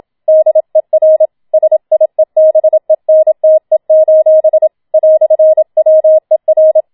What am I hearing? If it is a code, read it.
Morse code, heard as a keyed tone, about 27 wpm, 630 Hz. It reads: DER SIEBENTE: "WER
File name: Schneewittchen Teil2 27wpm